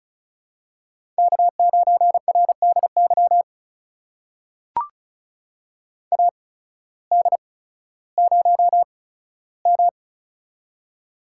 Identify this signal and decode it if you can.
Morse code, heard as a keyed tone, about 36 wpm, spaced as for 16 wpm, 700 Hz. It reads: K9RDY AD0M
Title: USA Call Signs - 8x Character Spacing - 35wpm